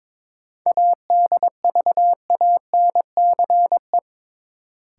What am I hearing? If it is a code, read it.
Morse code, heard as a keyed tone, about 22 wpm, 700 Hz. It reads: ADVANCE
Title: Sentences from Top 1400 Words - New Word Drill Alt - 22wpm